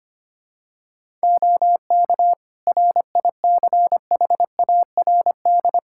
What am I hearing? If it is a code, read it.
Morse code, heard as a keyed tone, about 25 wpm, 700 Hz. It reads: OK RICHARD
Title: Practice QSO - Exchange 3 - More Info - 25wpm